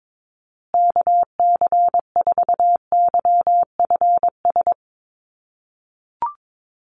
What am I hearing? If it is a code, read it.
Morse code, heard as a keyed tone, about 22 wpm, 700 Hz. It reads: KC4YFS
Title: USA Call Signs Repeated 3x 22wpm